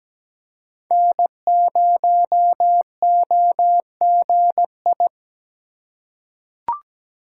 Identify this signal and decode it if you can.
Morse code, heard as a keyed tone, about 17 wpm, 700 Hz. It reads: N0OGI
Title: USA Call Signs - 5 Characters - 2x Character Spacing - 17wpm